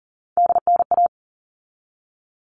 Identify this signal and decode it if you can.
Morse code, heard as a keyed tone, about 40 wpm, 700 Hz. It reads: DNA
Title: All Cut Numbers - 3-Digit 40wpm